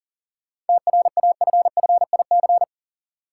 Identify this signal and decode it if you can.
Morse code, heard as a keyed tone, about 38 wpm, 700 Hz. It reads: TRAFFIC
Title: Sentences from Top 1300 Words - New Word Drill Alt - 40wpm